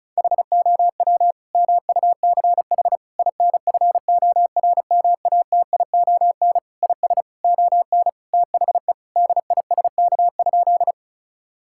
Morse code, heard as a keyed tone, about 36 wpm, 700 Hz. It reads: HOW MUCH INFORMATION IS ON THE DISK?
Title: Sentences from Top 1000 Words - Review Part 6 - 35wpm